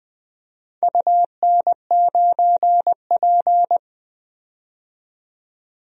Morse code, heard as a keyed tone, about 20 wpm, 700 Hz. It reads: UN9P
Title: DXCC-100 Call Signs - 4 Characters - 20wpm